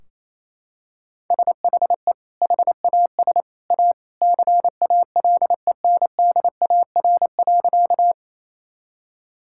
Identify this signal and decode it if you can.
Morse code, heard as a keyed tone, about 27 wpm, 700 Hz. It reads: SHE HAS A CALENDAR.
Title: Sentences from Top 1600 Words 28wpm